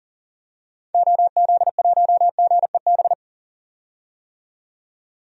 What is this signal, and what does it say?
Morse code, heard as a keyed tone, about 40 wpm, 700 Hz. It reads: OZ1GEB
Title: DXCC-100 Call Signs - 6 Characters - 40wpm